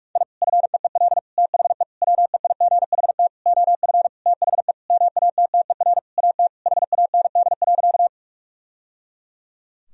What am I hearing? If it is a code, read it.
Morse code, heard as a keyed tone, about 41 wpm, 700 Hz. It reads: I FEEL THE WEIGHT OF THE MATTER AT HAND.
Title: Sentences from Top 1000 Words - Review Part 3 - 45wpm